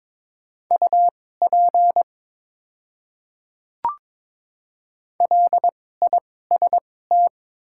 Morse code, heard as a keyed tone, about 22 wpm, spaced as for 16 wpm, 700 Hz. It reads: UP LIST
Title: 2x Character Spacing - Top 500 Words - 22wpm